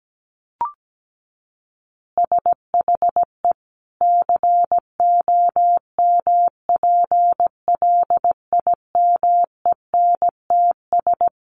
Morse code, heard as a keyed tone, about 17 wpm, 700 Hz. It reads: SHE COMPLIMENTS
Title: Easy Phrases 17wpm